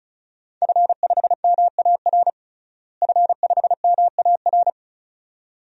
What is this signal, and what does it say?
Morse code, heard as a keyed tone, about 34 wpm, 700 Hz. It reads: F5MAR F5MAR
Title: DXCC-100 Call Signs - 5 Characters - Repeated 2x - 35wpm